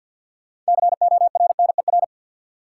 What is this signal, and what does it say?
Morse code, heard as a keyed tone, about 52 wpm, 700 Hz. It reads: CORNER
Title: Sentences from Top 1600 Words - New Word Drill - 50wpm